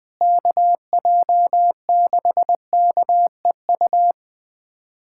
Morse code, heard as a keyed tone, about 20 wpm, 700 Hz. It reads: KJ6KEU
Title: USA Callsign DE Callsign - 6 Characters - 20wpm